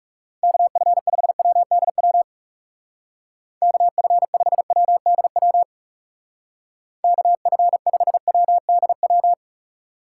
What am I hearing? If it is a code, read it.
Morse code, heard as a keyed tone, about 36 wpm, 700 Hz. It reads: KF5WDW KF5WDW KF5WDW
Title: Speed Racing - USA 2x3 Call Signs 30wpm